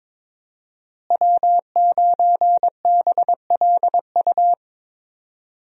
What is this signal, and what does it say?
Morse code, heard as a keyed tone, about 22 wpm, 700 Hz. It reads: W9BLU
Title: USA Call Signs - 5 Characters - Repeated 2x - 22wpm